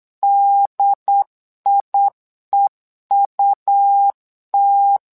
Morse code, heard as a keyed tone, about 8 wpm, 800 Hz. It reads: DIEUT
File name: LEZIONI / 2 UHDN / 2 UHD 8wpm 2258